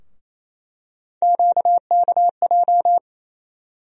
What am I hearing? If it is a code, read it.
Morse code, heard as a keyed tone, about 28 wpm, 700 Hz. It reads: QKJ
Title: USA Call Sign Postfixes - 3 Letters 28wpm